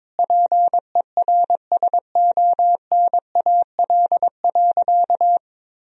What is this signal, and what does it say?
Morse code, heard as a keyed tone, about 22 wpm, 700 Hz. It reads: PERSONAL.
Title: Sentences from Top 1000 Words - Review Part 4 - 22wpm